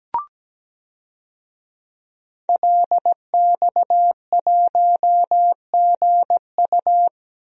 Morse code, heard as a keyed tone, about 17 wpm, 700 Hz. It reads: LX1GU
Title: DXCC-100 Call Signs - 5 Characters - 17wpm